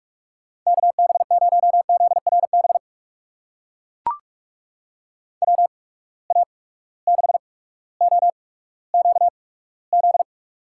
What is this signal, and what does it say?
Morse code, heard as a keyed tone, about 43 wpm, spaced as for 19 wpm, 700 Hz. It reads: KB0ZRB WA6OQZ
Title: USA Call Signs - 6 Characters - 8x Character Spacing - 45wpm